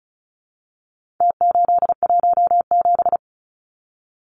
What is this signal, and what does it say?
Morse code, heard as a keyed tone, about 34 wpm, 700 Hz. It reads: T817
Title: Numbers - Contest Style - 4-Digit 35wpm